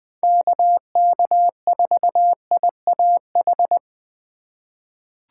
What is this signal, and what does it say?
Morse code, heard as a keyed tone, about 21 wpm, 700 Hz. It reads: KK4IAH
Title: USA Call Signs - 6 Characters - 6x Character Spacing - 20wpm